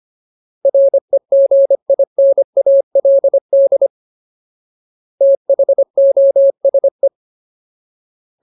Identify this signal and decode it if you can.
Morse code, heard as a keyed tone, about 25 wpm, 550 Hz. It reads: REGINALD THOSE
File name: Ultimate Word List / 25WPM Ultimate